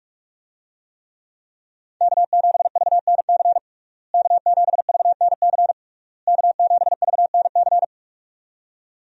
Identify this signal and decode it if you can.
Morse code, heard as a keyed tone, about 46 wpm, 700 Hz. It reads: K7VNC K7VNC K7VNC
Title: USA Call Signs - 5 Characters - Repeated 3x - 45wpm